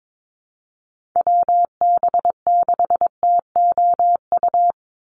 Morse code, heard as a keyed tone, about 22 wpm, 700 Hz. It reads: WB6TOU
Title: Call Signs Repeated 2x - 22wpm